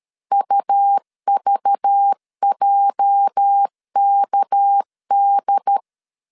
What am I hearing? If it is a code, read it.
Morse code, heard as a keyed tone, about 13 wpm, 800 Hz. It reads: UVJKD
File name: LEZIONI / 8 YJ45 / 8 YJ4 12wpm 2151